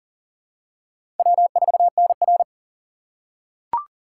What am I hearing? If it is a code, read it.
Morse code, heard as a keyed tone, about 40 wpm, 700 Hz. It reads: W4NR
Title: USA Call Signs - 2x Character Spacing - 40wpm